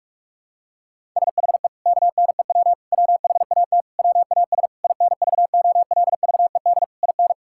Morse code, heard as a keyed tone, about 46 wpm, 700 Hz. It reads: SHE KNEW WHAT WAS INVOLVED IN
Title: Sentences from Top 1000 Words - Review Part 5 - 45wpm